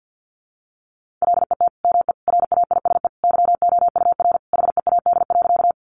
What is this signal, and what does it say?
Morse code, heard as a keyed tone, about 50 wpm, 700 Hz. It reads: LET ME RAISE YOUR HEAD.
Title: Sentences from Top 1400 Words 50wpm